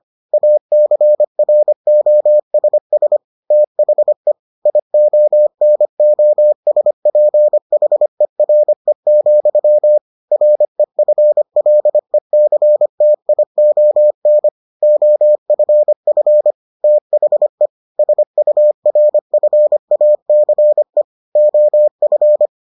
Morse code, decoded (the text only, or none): ACROSS THE IONOSPHERE, REFLECTION OFF THE SURFACE OF